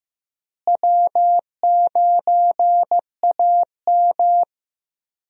Morse code, heard as a keyed tone, about 15 wpm, 700 Hz. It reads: W9AM
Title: USA Call Signs - 4 Characters - Repeated 3x - 15wpm